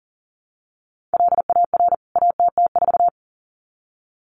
Morse code, heard as a keyed tone, about 40 wpm, 700 Hz. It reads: LAR ATT4
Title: CWT Exchange 40wpm